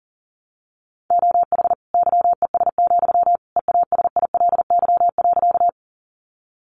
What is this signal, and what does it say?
Morse code, heard as a keyed tone, about 40 wpm, 700 Hz. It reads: OH YES, EASILY.
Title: Sentences from Top 900 Words 40wpm